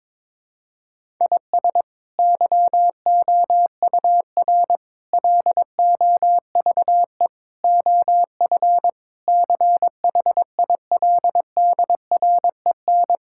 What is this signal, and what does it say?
Morse code, heard as a keyed tone, about 21 wpm, 700 Hz. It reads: IS YOUR LOVE OF CHILDREN
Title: Sentences from Top 1000 Words - Review Part 5 - 22wpm